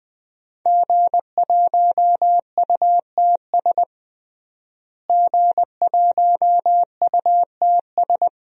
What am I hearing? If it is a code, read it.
Morse code, heard as a keyed tone, about 20 wpm, 700 Hz. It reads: G1UTS G1UTS
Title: DXCC-100 Call Signs - 5 Characters - Repeated 2x - 20wpm